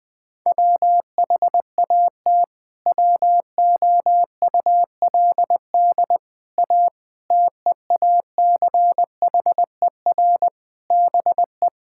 Morse code, heard as a keyed tone, about 21 wpm, 700 Hz. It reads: WHAT WOULD A TEACHER BE
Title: Sentences from Top 1000 Words - Review Part 6 - 20wpm